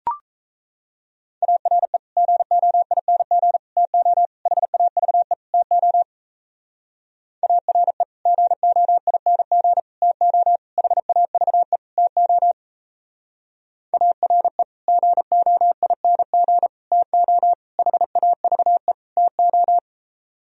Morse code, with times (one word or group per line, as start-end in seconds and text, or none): 1.4-2.0 ARE
2.2-3.6 GOING
3.8-4.3 TO
4.5-5.3 HAVE
5.5-6.0 TO
7.4-8.0 ARE
8.3-9.8 GOING
10.0-10.6 TO
10.8-11.8 HAVE
12.0-12.5 TO
13.9-14.6 ARE
14.9-16.7 GOING
16.9-17.5 TO
17.8-18.9 HAVE
19.2-19.8 TO